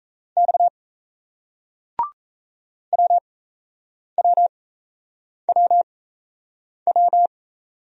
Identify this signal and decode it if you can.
Morse code, heard as a keyed tone, about 38 wpm, spaced as for 8 wpm, 700 Hz. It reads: X WWWW